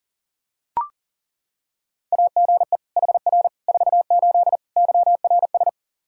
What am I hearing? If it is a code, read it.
Morse code, heard as a keyed tone, about 37 wpm, 700 Hz. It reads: AGE HR 48 YRS
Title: K1USN SST QSO Practice 40wpm